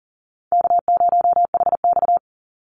Morse code, heard as a keyed tone, about 42 wpm, 700 Hz. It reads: K0HX
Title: Easy Call Signs - 40wpm